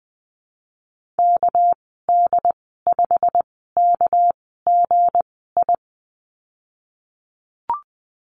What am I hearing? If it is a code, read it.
Morse code, heard as a keyed tone, about 20 wpm, spaced as for 15 wpm, 700 Hz. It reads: KD5KGI
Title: 219 Lesson 35g - Number 0 - Call Signs 20wpm